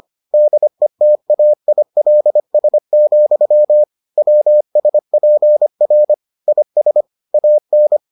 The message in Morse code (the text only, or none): DETAILS, WSPR IS AN